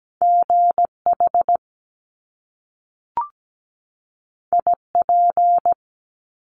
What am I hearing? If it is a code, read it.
Morse code, heard as a keyed tone, about 17 wpm, 700 Hz. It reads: GH IP
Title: Top Two Letter Combinations in Words 17wpm